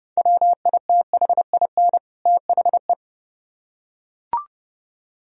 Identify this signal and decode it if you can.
Morse code, heard as a keyed tone, about 30 wpm, 700 Hz. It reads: WITHIN THE